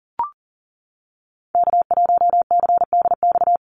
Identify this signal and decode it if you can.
Morse code, heard as a keyed tone, about 40 wpm, 700 Hz. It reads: K1CDX
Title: Call Signs - 40wpm